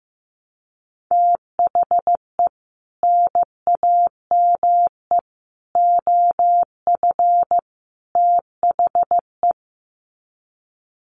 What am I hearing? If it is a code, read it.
Morse code, heard as a keyed tone, about 15 wpm, 700 Hz. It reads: THE NAME OF THE
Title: Sets of 4 Words 15wpm